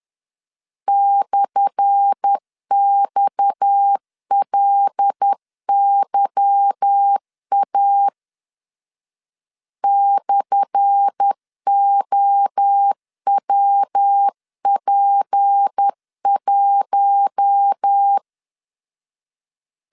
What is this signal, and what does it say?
Morse code, heard as a keyed tone, about 10 wpm, 800 Hz. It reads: /XLYA /OWP1
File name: LEZIONI / 10 SIGNS / 10A SLASH 10wpm 2808